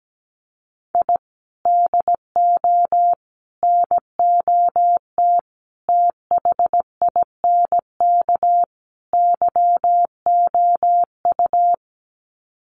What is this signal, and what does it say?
Morse code, heard as a keyed tone, about 17 wpm, 700 Hz. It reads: I DO NOT THINK YOU
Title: Sets of 5 Words 17wpm